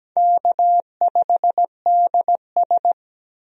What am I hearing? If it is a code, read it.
Morse code, heard as a keyed tone, about 17 wpm, 700 Hz. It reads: K5DS
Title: USA Call Signs - 4 Characters - Repeated 3x - 17wpm